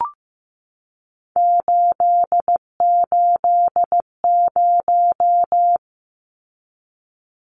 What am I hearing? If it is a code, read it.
Morse code, heard as a keyed tone, about 15 wpm, 700 Hz. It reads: 880
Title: Numbers - 3-Digit 15wpm